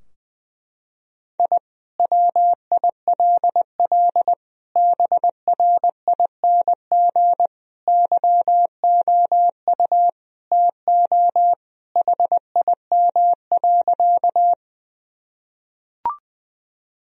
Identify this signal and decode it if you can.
Morse code, heard as a keyed tone, about 20 wpm, 700 Hz. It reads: I WILL BRING YOU TO HIM.
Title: Sentences from Top 1000 Words - Review Part 3 - 20wpm